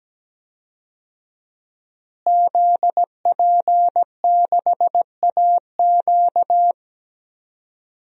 Morse code, heard as a keyed tone, about 17 wpm, 700 Hz. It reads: ZP6AQ
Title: DXCC-100 Call Signs - 5 Characters - Repeated 2x - 17wpm